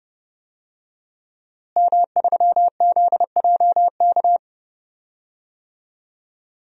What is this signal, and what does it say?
Morse code, heard as a keyed tone, about 31 wpm, 700 Hz. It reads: M3ZJK
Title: DXCC-100 Call Signs - 5 Characters - Repeated 2x - 30wpm